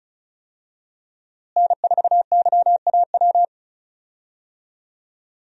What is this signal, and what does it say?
Morse code, heard as a keyed tone, about 36 wpm, 700 Hz. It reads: N4YAW
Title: USA Call Signs - 5 Characters - Repeated 3x - 35wpm